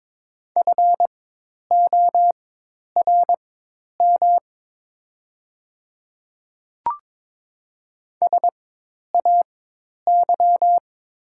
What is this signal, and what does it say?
Morse code, heard as a keyed tone, about 22 wpm, spaced as for 10 wpm, 700 Hz. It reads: FORM SAY
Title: 4x Character Spacing - Top 200 Words - 22wpm